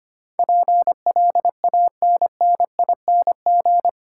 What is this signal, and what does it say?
Morse code, heard as a keyed tone, about 25 wpm, 700 Hz. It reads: PLANNING
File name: 8 Letter Words - First 3 Letters Given - 25wpm